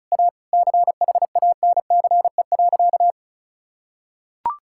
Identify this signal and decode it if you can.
Morse code, heard as a keyed tone, about 36 wpm, 700 Hz. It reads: A CHANCE.
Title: Sentences from Top 1000 Words - Review Part 5 - 35wpm